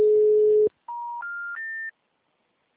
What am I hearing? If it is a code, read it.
Special information tone.